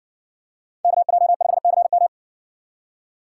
Morse code, heard as a keyed tone, about 55 wpm, 700 Hz. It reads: KJ5CM